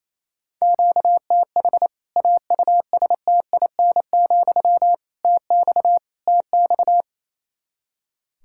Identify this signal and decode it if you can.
Morse code, heard as a keyed tone, about 28 wpm, 700 Hz. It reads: QTH AUSTIN, TX TX
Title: K1USN SST QSO Practice 28wpm